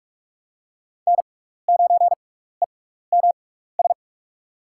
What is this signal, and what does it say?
Morse code, heard as a keyed tone, about 44 wpm, spaced as for 15 wpm, 700 Hz. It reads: N9EMS
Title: USA Call Signs - 5 Characters - 6x Character Spacing - 45wpm